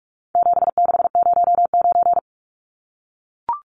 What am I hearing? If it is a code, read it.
Morse code, heard as a keyed tone, about 44 wpm, 700 Hz. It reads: 7609
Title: Numbers - 4-Digit 45wpm